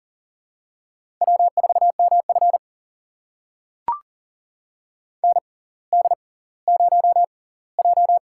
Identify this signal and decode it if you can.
Morse code, heard as a keyed tone, about 39 wpm, spaced as for 20 wpm, 690 Hz. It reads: W4MF ND0J
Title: USA Call Signs - 4 Characters - 6x Character Spacing - 40wpm